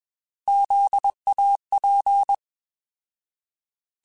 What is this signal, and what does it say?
Morse code, heard as a keyed tone, about 21 wpm, 800 Hz. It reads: ZAP
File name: Zs @20